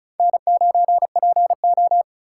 Morse code, heard as a keyed tone, about 49 wpm, spaced as for 42 wpm, 700 Hz. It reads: N9PO
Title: Speed Racing - USA 1x2 Call Signs 35wpm